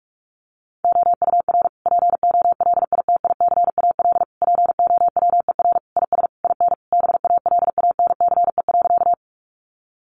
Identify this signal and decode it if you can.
Morse code, heard as a keyed tone, about 44 wpm, 700 Hz. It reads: OUR POLITICAL POWER IS IN BALANCE.
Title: Sentences from Top 900 Words 45wpm